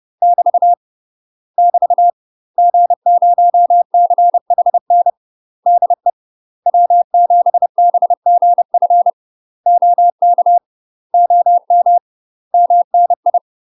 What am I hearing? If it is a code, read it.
Morse code, heard as a keyed tone, about 30 wpm, 700 Hz. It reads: <BT> <BT> G0CHN DE W7BGF OK OM MNI
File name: UK MASTER QSOs TRAINING / 30WPM / UK MASTER QSOs 0041